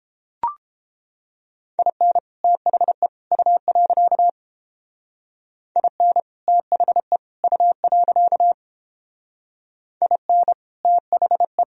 Morse code, heard as a keyed tone, about 31 wpm, 700 Hz. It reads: IN THE U. IN THE U. IN THE